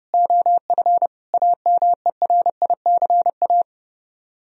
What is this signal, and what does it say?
Morse code, heard as a keyed tone, about 28 wpm, 700 Hz. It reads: OF AMERICA
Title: Speed-Racing - Sets of 5 Words 30wpm